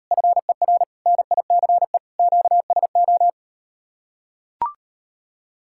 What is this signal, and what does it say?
Morse code, heard as a keyed tone, about 38 wpm, 700 Hz. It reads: FER NICE QSO